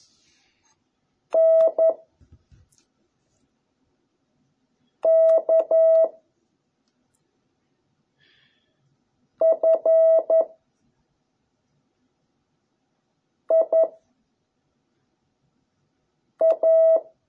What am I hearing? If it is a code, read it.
Morse code, heard as a keyed tone, about 11 wpm, 660 Hz. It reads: N K F I A